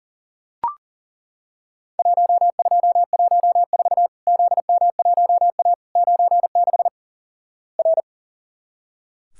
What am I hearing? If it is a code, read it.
Morse code, heard as a keyed tone, about 37 wpm, 700 Hz. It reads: 1214 ZM1A 96 R
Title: Worked All Europe DX Contest - QTC Exchange - 40wpm